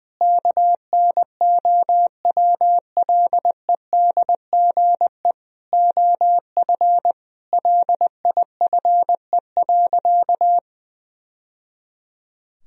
Morse code, heard as a keyed tone, about 20 wpm, 700 Hz. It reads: KNOWLEDGE OF LIFE.
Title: Sentences from Top 1000 Words - Review Part 3 - 20wpm